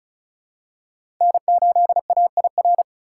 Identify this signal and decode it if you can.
Morse code, heard as a keyed tone, about 36 wpm, 700 Hz. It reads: N8AIR
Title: USA Call Signs - 5 Characters - 2x Character Spacing - 35wpm